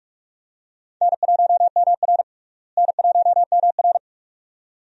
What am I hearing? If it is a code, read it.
Morse code, heard as a keyed tone, about 41 wpm, 700 Hz. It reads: N1MR N1MR
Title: USA Call Signs - 4 Characters - Repeated 2x - 45wpm